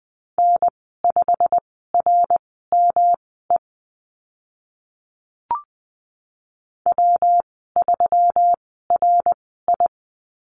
Morse code, heard as a keyed tone, about 20 wpm, spaced as for 15 wpm, 700 Hz. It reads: N5RME W3RI
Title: 140 Lesson 23g - Letter f - Call Signs 20wpm